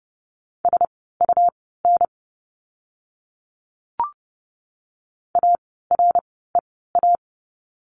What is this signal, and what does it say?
Morse code, heard as a keyed tone, about 30 wpm, spaced as for 17 wpm, 700 Hz. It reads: SUN AREA
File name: Lesson 17f - Letter u - Words 30wpm